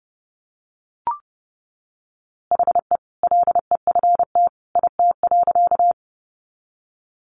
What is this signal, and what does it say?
Morse code, heard as a keyed tone, about 30 wpm, 700 Hz. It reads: HE LEFT IT.
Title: A Sound Of Thunder 30wpm